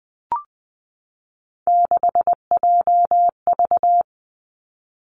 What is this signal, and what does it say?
Morse code, heard as a keyed tone, about 20 wpm, 700 Hz. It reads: 6JV
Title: Three Challenging Letter-Number Combinations 20wpm